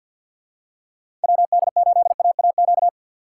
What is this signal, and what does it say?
Morse code, heard as a keyed tone, about 50 wpm, 700 Hz. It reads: WD8AAQ
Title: USA Call Signs - 6 Characters - 4x Character Spacing - 50wpm